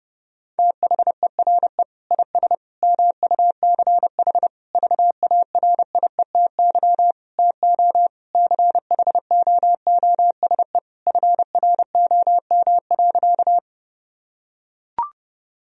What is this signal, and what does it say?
Morse code, heard as a keyed tone, about 32 wpm, 700 Hz. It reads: THERE IS MUCH VARIETY TO CHOOSE FROM.
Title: Sentences from Top 1000 Words - Review Part 4 - 30wpm